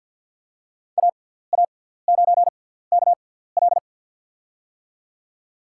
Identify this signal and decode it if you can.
Morse code, heard as a keyed tone, about 53 wpm, spaced as for 17 wpm, 700 Hz. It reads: AA9KL
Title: USA Call Signs - 5 Characters - 6x Character Spacing - 50wpm